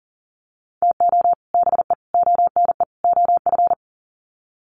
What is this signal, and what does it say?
Morse code, heard as a keyed tone, about 38 wpm, 700 Hz. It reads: TO BE ONE OF
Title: Sets of 4 Words 40wpm